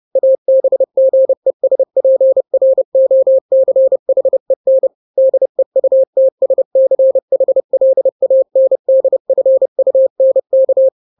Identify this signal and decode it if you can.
Morse code, heard as a keyed tone, about 30 wpm, 530 Hz. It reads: ABGESPROCHEN DEUTSCHLANDFUNK
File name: daily / 29wpm